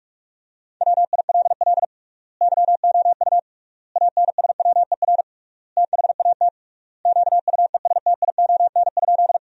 Morse code, heard as a keyed tone, about 46 wpm, 700 Hz. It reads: WILL YOU ANSWER THAT QUESTION?
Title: 3x Word-Spacing - Sentences from Top 400 Words - 45wpm